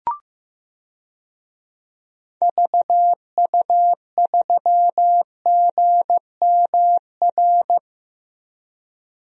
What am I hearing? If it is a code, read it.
Morse code, heard as a keyed tone, about 15 wpm, 700 Hz. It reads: VU3GMR
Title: DXCC-100 Call Signs - 6 Characters - Repeated 2x - 15wpm